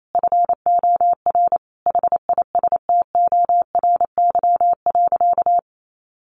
Morse code, heard as a keyed tone, about 28 wpm, 700 Hz. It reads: FOR HISTORY.
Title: Sentences from Top 600 Words 28wpm